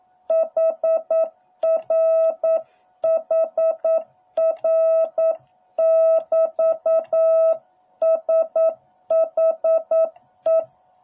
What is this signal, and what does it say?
Morse code, heard as a keyed tone, about 9 wpm, 660 Hz. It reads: HR HR <BT> SHE